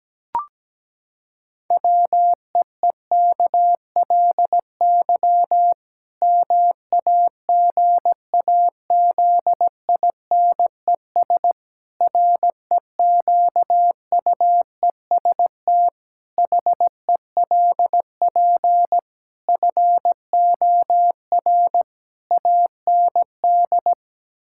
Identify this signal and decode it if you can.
Morse code, heard as a keyed tone, about 17 wpm, 700 Hz. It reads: WEEKLY MAGAZINES REQUEST HELP FOR AND
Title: Pangram Challenge v2 - 17wpm